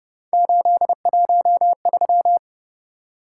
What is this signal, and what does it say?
Morse code, heard as a keyed tone, about 29 wpm, 700 Hz. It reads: 813